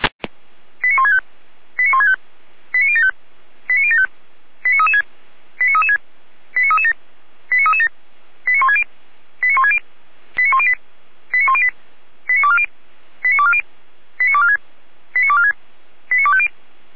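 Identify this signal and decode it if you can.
Five-tone selective call (ZVEI), heard as quick runs of five tones, tones of about 70 ms. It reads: 89165 89165 89075 89075 89336 89336 89337 89337 89160 89160 89108 89108 89240 89240 89246 89246 89250
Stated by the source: ZVEI five-tone selective call received via SDR.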